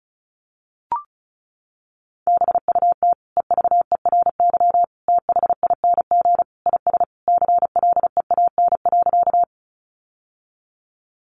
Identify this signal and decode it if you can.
Morse code, heard as a keyed tone, about 34 wpm, 700 Hz. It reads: BUT EVERYTHING IS CLEAN.
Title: Sentences from Top 1100 Words 35wpm